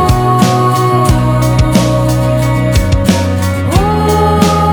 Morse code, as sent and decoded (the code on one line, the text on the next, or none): none